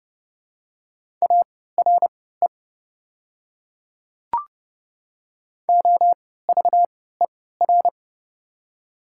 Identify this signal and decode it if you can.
Morse code, heard as a keyed tone, about 30 wpm, spaced as for 17 wpm, 700 Hz. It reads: ARE OVER
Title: 3x Character Spacing - Top 200 Words - 30wpm